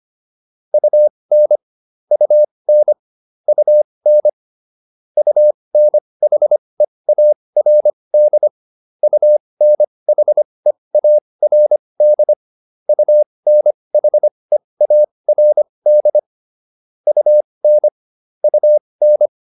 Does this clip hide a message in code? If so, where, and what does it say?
Yes: Morse code from 0.7 to 19.3 s, heard as a keyed tone, about 25 wpm, spaced as for 20 wpm, 600 Hz. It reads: UN UN UN UNHEARD UNHEARD UNHEARD UN UN